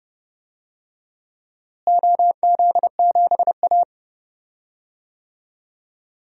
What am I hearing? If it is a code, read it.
Morse code, heard as a keyed tone, about 30 wpm, 700 Hz. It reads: OZ7A
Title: DXCC-100 Call Signs - 4 Characters - Repeated 3x - 30wpm